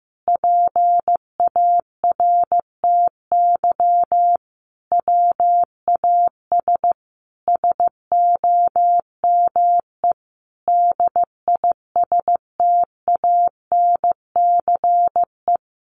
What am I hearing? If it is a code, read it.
Morse code, heard as a keyed tone, about 15 wpm, 700 Hz. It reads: PARTY WAS SOME DISTANCE
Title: Sentences from Top 1400 Words 15wpm